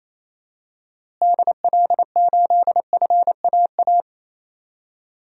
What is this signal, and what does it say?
Morse code, heard as a keyed tone, about 27 wpm, 700 Hz. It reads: DL8FAA
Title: DXCC-100 Call Signs - 6 Characters - Repeated 2x - 28wpm